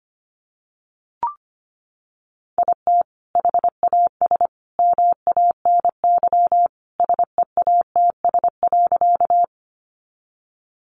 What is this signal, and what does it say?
Morse code, heard as a keyed tone, about 25 wpm, 700 Hz. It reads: IT HAS MANY SEATS.
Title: Additional Easy Phrases 25wpm